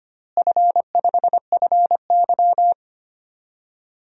Morse code, heard as a keyed tone, about 25 wpm, 700 Hz. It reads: F5FY
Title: DXCC-100 Call Signs - 4 Characters - Repeated 3x - 25wpm